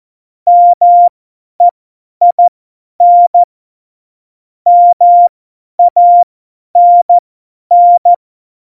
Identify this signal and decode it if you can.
Morse code, heard as a keyed tone, about 14 wpm, spaced as for 10 wpm, 700 Hz. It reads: MEIN MANN
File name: Morsekurs / Witze / Witz 22. August 17, Tempo 50 F70